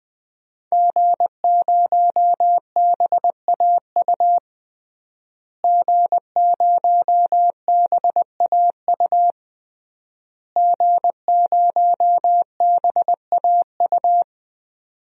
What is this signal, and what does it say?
Morse code, heard as a keyed tone, about 21 wpm, 700 Hz. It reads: G0BAU G0BAU G0BAU
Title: DXCC-100 Call Signs - 5 Characters - Repeated 3x - 20wpm